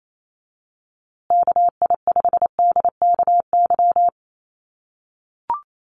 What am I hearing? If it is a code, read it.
Morse code, heard as a keyed tone, about 28 wpm, 700 Hz. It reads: KI5DKY
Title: USA Call Signs Repeated 2x 28wpm